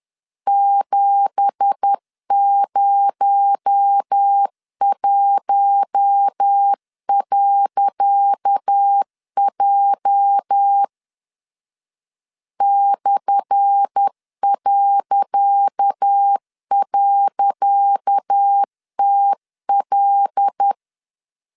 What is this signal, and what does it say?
Morse code, heard as a keyed tone, about 10 wpm, 800 Hz. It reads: 701.J /..TL
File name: LEZIONI / 10 SIGNS / 10C DOT 10wpm 2940